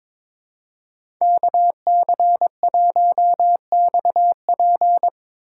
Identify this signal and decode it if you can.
Morse code, heard as a keyed tone, about 25 wpm, 700 Hz. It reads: KC1XP
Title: Speed Racing - USA Call Signs 22wpm